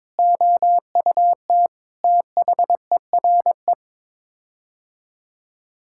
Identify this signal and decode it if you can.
Morse code, heard as a keyed tone, about 22 wpm, 700 Hz. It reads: OUT THERE
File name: practice / 22wpm / Sets of 8 Words - 22wpm